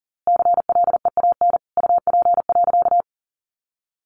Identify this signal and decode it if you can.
Morse code, heard as a keyed tone, about 40 wpm, 700 Hz. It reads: CLEAN UP.
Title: A Sound Of Thunder 40wpm